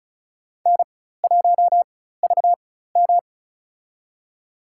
Morse code, heard as a keyed tone, about 36 wpm, spaced as for 16 wpm, 700 Hz. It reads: N1VM
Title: USA Call Signs - 4x Character Spacing - 35wpm